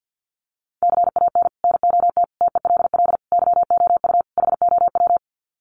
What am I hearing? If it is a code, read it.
Morse code, heard as a keyed tone, about 48 wpm, 700 Hz. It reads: CAN NOT TELL YOU HOW
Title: Sets of 5 Words Encore2 50wpm